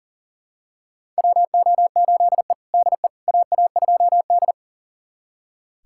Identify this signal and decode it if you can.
Morse code, heard as a keyed tone, about 40 wpm, 700 Hz. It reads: WO8E DE AA2D
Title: USA Callsign DE Callsign - 4 Characters - 40wpm